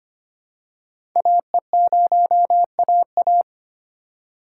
Morse code, heard as a keyed tone, about 25 wpm, 700 Hz. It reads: AE0AA
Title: USA Call Signs - 5 Characters - 4x Character Spacing - 25wpm